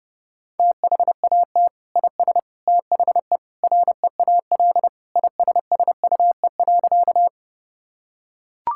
Morse code, heard as a keyed tone, about 30 wpm, 700 Hz. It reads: THAT IS THE REAL ISSUE.
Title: Sentences from Top 1000 Words - Review Part 6 - 30wpm